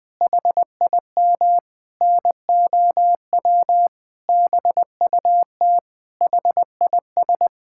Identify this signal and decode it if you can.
Morse code, heard as a keyed tone, about 20 wpm, 700 Hz. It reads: HIM NOW BUT HIS
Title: Sentences from Top 1000 Words - Review Part 6 - 20wpm